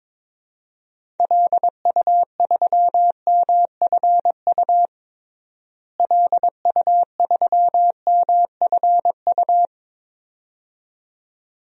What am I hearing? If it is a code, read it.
Morse code, heard as a keyed tone, about 22 wpm, 700 Hz. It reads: LU3MFU LU3MFU
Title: DXCC-100 Call Signs - 6 Characters - Repeated 2x - 22wpm